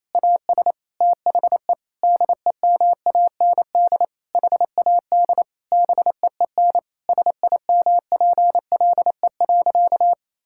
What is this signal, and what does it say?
Morse code, heard as a keyed tone, about 28 wpm, 700 Hz. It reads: AS THE DEMAND HAD BEEN SIMPLE.
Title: Sentences from Top 1000 Words - Review Part 6 - 28wpm